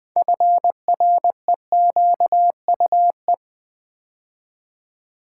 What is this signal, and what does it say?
Morse code, heard as a keyed tone, about 20 wpm, 690 Hz. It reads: FREQUE
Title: Build A Word - 3 Letters At A Time - 9-Letter Words - 20wpm